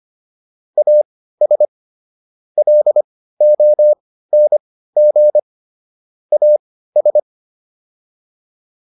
Morse code, heard as a keyed tone, about 25 wpm, spaced as for 15 wpm, 600 Hz. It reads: AS LONG AS